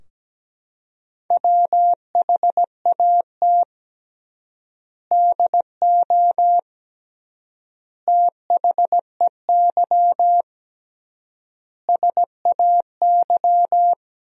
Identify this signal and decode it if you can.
Morse code, heard as a keyed tone, about 17 wpm, 700 Hz. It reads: WHAT DO THEY SAY
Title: 3x Word-Spacing - Sentences from Top 200 Words - 17wpm